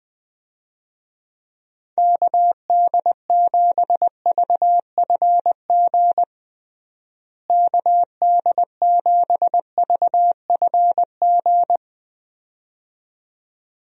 Morse code, heard as a keyed tone, about 20 wpm, 700 Hz. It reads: KD7VFG KD7VFG
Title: USA Call Signs - 6 Characters - Repeated 2x - 20wpm